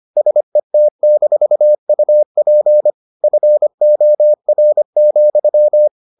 Morse code, heard as a keyed tone, about 25 wpm, 600 Hz. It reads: SET-UP FOR,